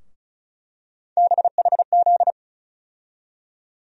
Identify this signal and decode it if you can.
Morse code, heard as a keyed tone, about 36 wpm, 700 Hz. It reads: BHZ
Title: USA Call Sign Postfixes - 3 Letters 35wpm